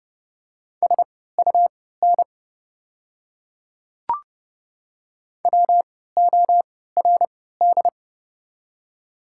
Morse code, heard as a keyed tone, about 29 wpm, spaced as for 17 wpm, 690 Hz. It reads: SUN WORD
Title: 3x Character Spacing - Top 300 Words - 30wpm